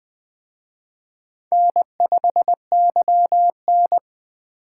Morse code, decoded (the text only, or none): N5YN